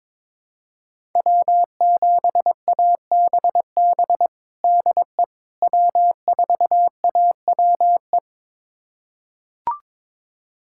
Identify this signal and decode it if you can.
Morse code, heard as a keyed tone, about 22 wpm, 700 Hz. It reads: W7ABB DE W4AWE
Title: USA Callsign DE Callsign - 5 Characters - 22wpm